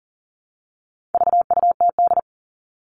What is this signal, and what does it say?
Morse code, heard as a keyed tone, about 40 wpm, 700 Hz. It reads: VUTD
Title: All Cut Numbers - 4-Digit 40wpm